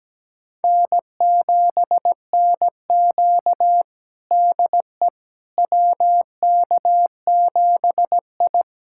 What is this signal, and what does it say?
Morse code, heard as a keyed tone, about 17 wpm, 700 Hz. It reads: N7NQ DE WK7I
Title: USA Callsign DE Callsign - 4 Characters - 17wpm